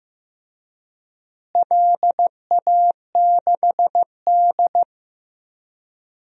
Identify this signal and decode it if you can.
Morse code, heard as a keyed tone, about 15 wpm, 700 Hz. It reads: LA6D
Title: DXCC-100 Call Signs - 4 Characters - Repeated 3x - 15wpm